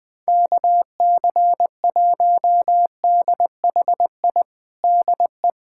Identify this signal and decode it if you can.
Morse code, heard as a keyed tone, about 21 wpm, 700 Hz. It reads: KC1DHI DE
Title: USA Callsign DE Callsign - 6 Characters - 20wpm